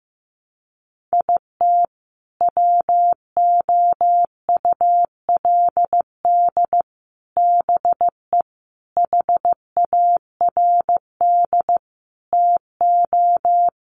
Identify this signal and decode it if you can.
Morse code, heard as a keyed tone, about 15 wpm, 700 Hz. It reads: IT WOULD BE HARD TO
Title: Sets of 5 Words Encore2 15wpm